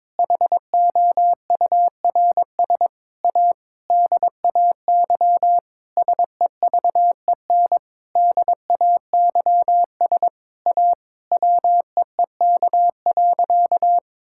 Morse code, heard as a keyed tone, about 22 wpm, 700 Hz. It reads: HOURS A DAY SEVEN DAYS A WEEK.
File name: Sentences with 7-Word Phrases - 22wpm